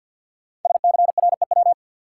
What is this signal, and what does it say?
Morse code, heard as a keyed tone, about 47 wpm, 700 Hz. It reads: SCREW
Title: Five Letter Words - 50wpm